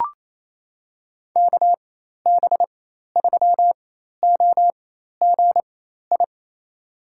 Morse code, heard as a keyed tone, about 28 wpm, spaced as for 13 wpm, 700 Hz. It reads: KB3OGI
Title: USA Call Signs - 6 Characters - 4x Character Spacing - 28wpm